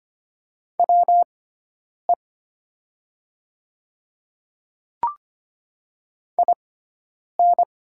Morse code, heard as a keyed tone, about 25 wpm, spaced as for 9 wpm, 690 Hz. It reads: WE IN
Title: 6x Character Spacing - Top 200 Words - 25wpm